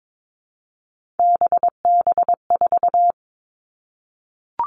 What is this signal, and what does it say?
Morse code, heard as a keyed tone, about 22 wpm, 700 Hz. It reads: BB4
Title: All Cut Numbers - 3-Digit 22wpm